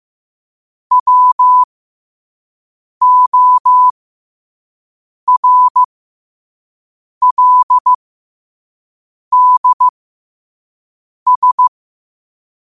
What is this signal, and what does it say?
Morse code, heard as a keyed tone, about 15 wpm, spaced as for 5 wpm, 1000 Hz. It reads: WORLDS
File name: qotdfiles / 5wpm